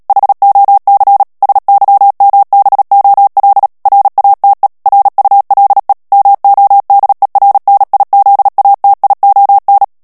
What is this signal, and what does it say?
Morse code, heard as a keyed tone, about 37 wpm, 800 Hz. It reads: HOC SYMBOL RATE RULE MODERNIZATION
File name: morse mp3 / ANGLAIS texte clair 3 vitesse 35 mots minute